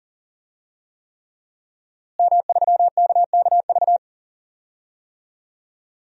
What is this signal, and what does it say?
Morse code, heard as a keyed tone, about 40 wpm, 700 Hz. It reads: M3KKV
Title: DXCC-100 Call Signs - 5 Characters - Repeated 3x - 40wpm